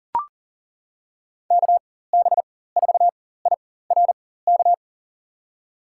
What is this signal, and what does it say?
Morse code, heard as a keyed tone, about 40 wpm, spaced as for 19 wpm, 700 Hz. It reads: KB4IRK
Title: USA Call Signs - 6 Characters - 4x Character Spacing - 40wpm